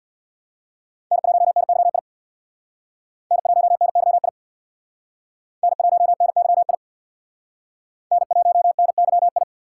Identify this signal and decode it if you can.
Morse code, heard as a keyed tone, about 61 wpm, 700 Hz. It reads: N1NYI N1NYI N1NYI N1NYI
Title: Speed Racing - USA 1x3 Call Signs 50wpm